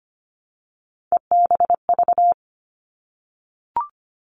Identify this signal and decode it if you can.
Morse code, heard as a keyed tone, about 25 wpm, 700 Hz. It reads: EBV
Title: Three Challenging Letter-Number Combinations 25wpm